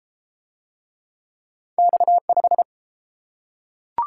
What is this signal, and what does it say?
Morse code, heard as a keyed tone, about 30 wpm, 700 Hz. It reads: X5